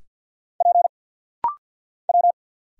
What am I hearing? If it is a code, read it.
Morse code, heard as a keyed tone, about 50 wpm, 700 Hz. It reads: P W
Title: Single Letters - Rapid-Fire 50wpm